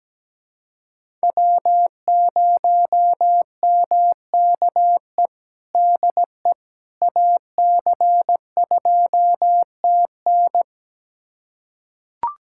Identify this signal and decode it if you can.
Morse code, heard as a keyed tone, about 17 wpm, 700 Hz. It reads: W0MKE DE AC2TN
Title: USA Callsign DE Callsign - 5 Characters - 17wpm